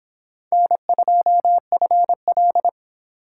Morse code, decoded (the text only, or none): N2FL